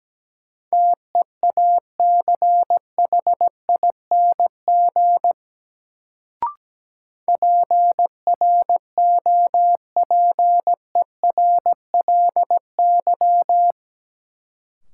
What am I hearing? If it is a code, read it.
Morse code, heard as a keyed tone, about 17 wpm, 700 Hz. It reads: TEACHING PROPERLY
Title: Sentences from Top 1100 Words - New Word Drill - 17wpm